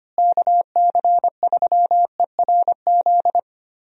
Morse code, heard as a keyed tone, about 25 wpm, 700 Hz. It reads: KC3ERZ
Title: USA Call Signs - 6 Characters - Repeated 3x - 25wpm